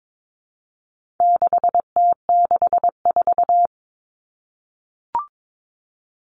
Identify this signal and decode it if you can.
Morse code, heard as a keyed tone, about 22 wpm, 700 Hz. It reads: 6T64
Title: All Cut Numbers - 4-Digit 22wpm